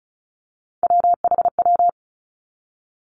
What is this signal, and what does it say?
Morse code, heard as a keyed tone, about 36 wpm, 700 Hz. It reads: WHW
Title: Three Challenging Letter-Number Combinations 35wpm